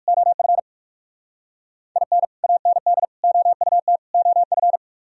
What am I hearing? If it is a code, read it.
Morse code, heard as a keyed tone, about 57 wpm, 700 Hz. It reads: OF IN AND OUT OF